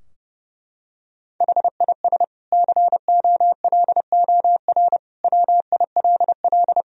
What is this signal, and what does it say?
Morse code, heard as a keyed tone, about 30 wpm, 700 Hz. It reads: HIS COLOR WILL
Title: Sentences from Top 1000 Words - Review Part 2 - 30wpm